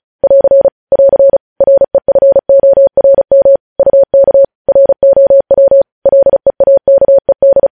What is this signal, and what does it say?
Morse code, heard as a keyed tone, about 36 wpm, 550 Hz. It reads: <AR> <AR> REFORM UK ROW LEAKED
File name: podcast / News headlines-1x-35x35